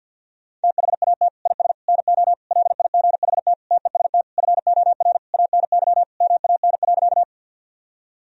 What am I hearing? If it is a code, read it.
Morse code, heard as a keyed tone, about 50 wpm, 700 Hz. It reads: THAT IS NO LIGHT TEST FOR ANY MAN.
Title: Sentences from Top 1000 Words - Review Part 3 - 50wpm